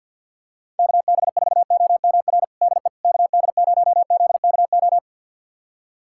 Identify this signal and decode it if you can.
Morse code, heard as a keyed tone, about 49 wpm, 700 Hz. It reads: KB4OMR DE KD0ZKO
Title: USA Callsign DE Callsign - 6 Characters - 50wpm